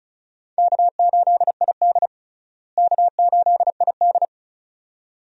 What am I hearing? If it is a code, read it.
Morse code, heard as a keyed tone, about 34 wpm, 700 Hz. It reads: K8ID K8ID
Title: USA Call Signs - 4 Characters - Repeated 2x - 35wpm